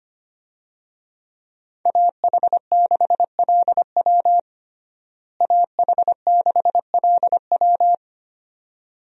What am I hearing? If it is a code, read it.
Morse code, heard as a keyed tone, about 25 wpm, 700 Hz. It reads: AH6LW AH6LW
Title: USA Call Signs - 5 Characters - Repeated 2x - 25wpm